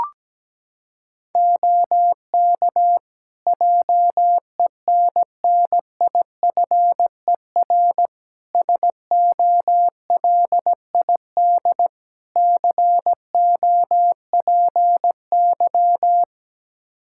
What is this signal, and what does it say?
Morse code, heard as a keyed tone, about 17 wpm, 700 Hz. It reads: OK JENNIFER SOLID COPY
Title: Practice QSO - Exchange 3 - More Info - 17wpm